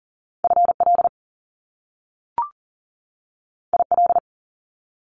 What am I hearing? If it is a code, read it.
Morse code, heard as a keyed tone, about 39 wpm, 700 Hz. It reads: FL IL
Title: Top Two Letter Combinations in Words 40wpm